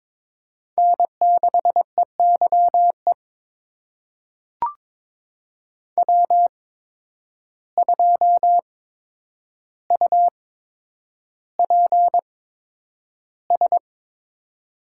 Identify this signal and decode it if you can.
Morse code, heard as a keyed tone, about 22 wpm, spaced as for 9 wpm, 700 Hz. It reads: N6EYE W2UPS
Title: USA Call Signs - 5 Characters - 8x Character Spacing - 22wpm